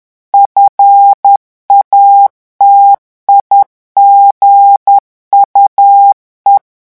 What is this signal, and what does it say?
Morse code, heard as a keyed tone, about 10 wpm, 800 Hz. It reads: FATIGUE
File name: revision9 citations humoristiques vitesse 10 mots minute